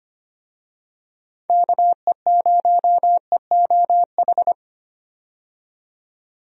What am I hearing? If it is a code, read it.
Morse code, heard as a keyed tone, about 25 wpm, 700 Hz. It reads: KE0EOH
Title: USA Call Signs - 6 Characters - 25wpm